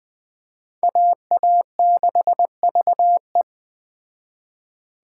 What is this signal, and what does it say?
Morse code, heard as a keyed tone, about 21 wpm, 700 Hz. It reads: AA6VE
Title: Speed Racing - USA Call Signs 17wpm